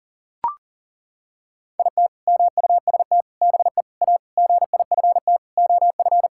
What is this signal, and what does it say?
Morse code, heard as a keyed tone, about 40 wpm, 700 Hz. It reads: IT MUST BE A GIFT OF
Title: Sentences from Top 1000 Words - Review Part 6 - 40wpm